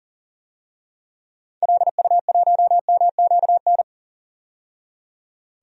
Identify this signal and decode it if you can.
Morse code, heard as a keyed tone, about 40 wpm, 700 Hz. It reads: LU1MQN
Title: DXCC-100 Call Signs - 6 Characters - 40wpm